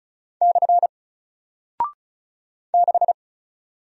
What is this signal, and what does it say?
Morse code, heard as a keyed tone, about 34 wpm, 700 Hz. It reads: / 6
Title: Single Character (Letter:Number:Punctuation:Prosign) Alt 35wpm